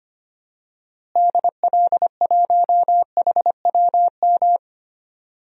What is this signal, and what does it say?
Morse code, heard as a keyed tone, about 25 wpm, 700 Hz. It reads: DL1HWM
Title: DXCC-100 Call Signs - 6 Characters - Repeated 3x - 25wpm